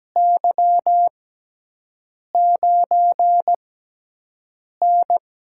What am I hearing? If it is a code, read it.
Morse code, heard as a keyed tone, about 17 wpm, spaced as for 6 wpm, 700 Hz. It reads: Y9N
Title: USA Call Signs - 6x Character Spacing - 17wpm